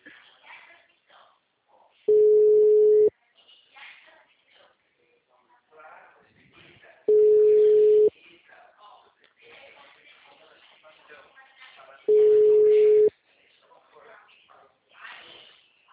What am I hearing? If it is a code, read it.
Ringback tone (Europe, 425 Hz), with speech.
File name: out- -2011- - - .31642